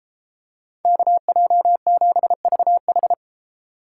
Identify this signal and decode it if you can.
Morse code, heard as a keyed tone, about 32 wpm, 700 Hz. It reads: KJ7VH